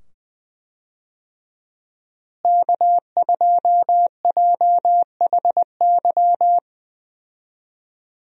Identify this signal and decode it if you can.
Morse code, heard as a keyed tone, about 20 wpm, 700 Hz. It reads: K2JHY
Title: USA Call Signs - 5 Characters - Repeated 2x - 20wpm